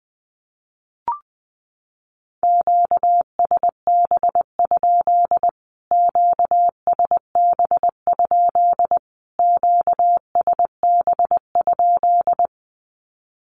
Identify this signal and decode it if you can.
Morse code, heard as a keyed tone, about 20 wpm, 700 Hz. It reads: QSB? QSB? QSB?
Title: Q-Codes Repeated 3x 20wpm